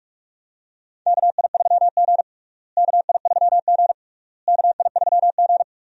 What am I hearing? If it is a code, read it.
Morse code, heard as a keyed tone, about 46 wpm, 700 Hz. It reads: KI3G KI3G KI3G
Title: USA Call Signs - 4 Characters - Repeated 3x - 45wpm